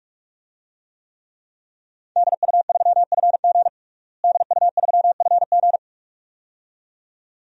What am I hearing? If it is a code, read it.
Morse code, heard as a keyed tone, about 45 wpm, 700 Hz. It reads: DU3FG DU3FG
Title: DXCC-100 Call Signs - 5 Characters - Repeated 2x - 45wpm